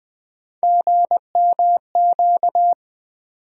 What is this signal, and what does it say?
Morse code, heard as a keyed tone, about 21 wpm, 700 Hz. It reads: GMQ
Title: USA Call Sign Postfixes - 3 Letters 20wpm